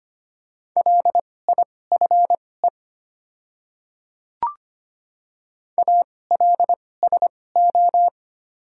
Morse code, heard as a keyed tone, about 25 wpm, spaced as for 18 wpm, 700 Hz. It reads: LIFE ALSO
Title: 2x Character Spacing - Top 300 Words - 25wpm